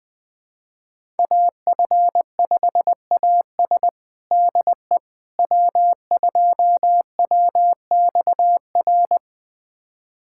Morse code, heard as a keyed tone, about 20 wpm, 700 Hz. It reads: AF5AS DE W2WXR
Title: USA Callsign DE Callsign - 5 Characters - 20wpm